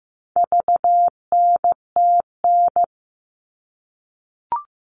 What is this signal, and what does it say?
Morse code, heard as a keyed tone, about 15 wpm, 700 Hz. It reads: VNTN
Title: All Cut Numbers - 4-Digit 15wpm